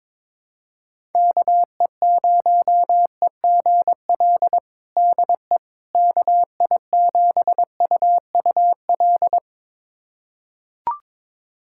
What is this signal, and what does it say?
Morse code, heard as a keyed tone, about 22 wpm, 700 Hz. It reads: KE0EGL DE KI7UUL
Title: USA Callsign DE Callsign 22wpm